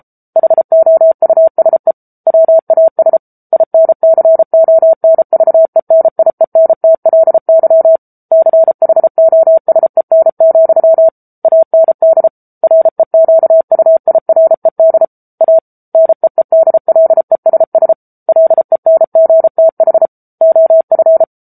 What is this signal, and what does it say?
Morse code, heard as a keyed tone, about 34 wpm, 650 Hz. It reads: HOUSE WAS INCONVENIENTLY CHOSEN, AND REQUIRED A NEEDLESS LENGTH OF
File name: Book Two / 07-Chapter 7 - 33 WPM